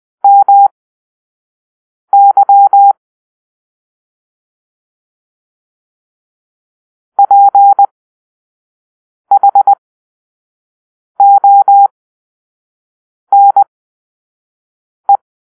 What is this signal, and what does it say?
Morse code, heard as a keyed tone, about 20 wpm, spaced as for 5 wpm, 800 Hz. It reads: MY PHONE
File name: arrl / Grp-6 Drill-4